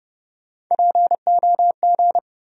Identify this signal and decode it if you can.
Morse code, heard as a keyed tone, about 30 wpm, 700 Hz. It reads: POG
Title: USA Call Sign Postfixes - 3 Letters 30wpm